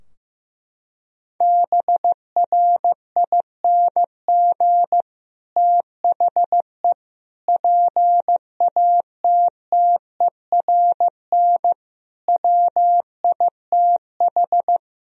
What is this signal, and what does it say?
Morse code, heard as a keyed tone, about 15 wpm, 700 Hz. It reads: BRING THE PATTERN WITH
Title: Sentences from Top 1000 Words - Review Part 2 - 15wpm